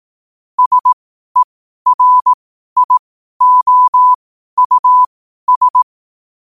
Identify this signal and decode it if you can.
Morse code, heard as a keyed tone, about 18 wpm, spaced as for 12 wpm, 1000 Hz. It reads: SERIOUS
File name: qotdfiles / 12wpm